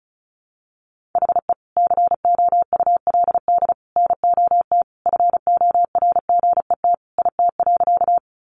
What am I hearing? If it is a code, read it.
Morse code, heard as a keyed tone, about 34 wpm, 700 Hz. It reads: HE COULD NOT FORGET IT.
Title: Sentences from Top 1200 Words 35wpm